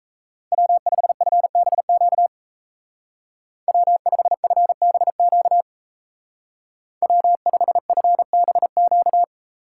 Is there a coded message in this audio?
Yes: Morse code from 0.5 to 9.2 s, heard as a keyed tone, about 37 wpm, 700 Hz. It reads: W5FBQ W5FBQ W5FBQ